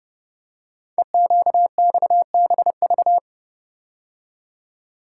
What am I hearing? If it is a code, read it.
Morse code, heard as a keyed tone, about 30 wpm, 700 Hz. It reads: EQXBV
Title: Code Groups - 5 Letter:Numbers - 30wpm